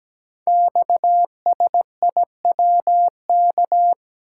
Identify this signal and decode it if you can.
Morse code, heard as a keyed tone, about 17 wpm, 700 Hz. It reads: XSIWK
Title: Code Groups - 5 Letters - 17wpm